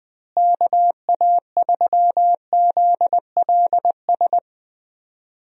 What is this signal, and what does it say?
Morse code, heard as a keyed tone, about 20 wpm, 700 Hz. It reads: KA3ZLS
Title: USA Call Signs - 6 Characters - Repeated 3x - 20wpm